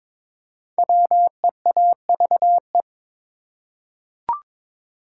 Morse code, heard as a keyed tone, about 22 wpm, 700 Hz. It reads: WEAVE
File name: 5 Letter Words - First 3 Letters Given - 22wpm